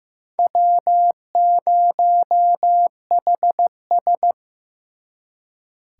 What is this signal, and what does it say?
Morse code, heard as a keyed tone, about 18 wpm, 700 Hz. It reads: W0HS
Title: Speed Racing - USA 1x2 Call Signs 15wpm